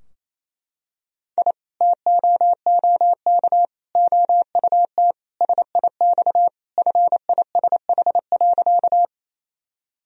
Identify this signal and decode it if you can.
Morse code, heard as a keyed tone, about 28 wpm, 700 Hz. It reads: I TOOK OUT SIX FISH.
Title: Sentences from Top 1000 Words - Review Part 2 - 28wpm